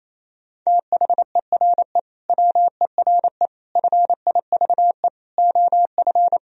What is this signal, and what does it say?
Morse code, heard as a keyed tone, about 28 wpm, 700 Hz. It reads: THERE WERE FIVE OF
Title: Sentences from Top 1000 Words - Review Part 2 - 28wpm